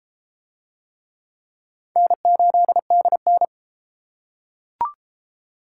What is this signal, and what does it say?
Morse code, heard as a keyed tone, about 29 wpm, 700 Hz. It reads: N8DN